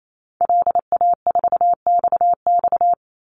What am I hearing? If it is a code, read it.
Morse code, heard as a keyed tone, about 28 wpm, 700 Hz. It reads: LA4XX
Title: Call Signs - 28wpm